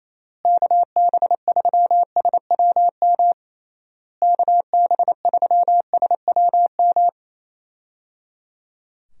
Morse code, heard as a keyed tone, about 28 wpm, 700 Hz. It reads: KB3SWM KB3SWM
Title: USA Call Signs - 6 Characters - Repeated 3x - 28wpm